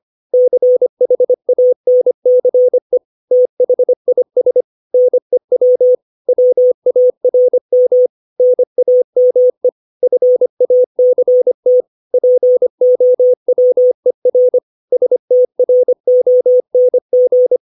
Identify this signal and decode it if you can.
Morse code, heard as a keyed tone, about 25 wpm, 500 Hz. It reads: CHANCE THIS NEW WARM NAME FACT POWER STRONG
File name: 632 words / 632 Words 25WPM